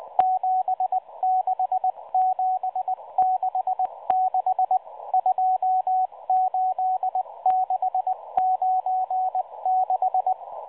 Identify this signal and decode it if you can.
Morse code, heard as a keyed tone, about 18 wpm, 750 Hz, with radio noise. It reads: 76766 28696
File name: HM02 5-14-15 0140 AM UTC 7351kHz CW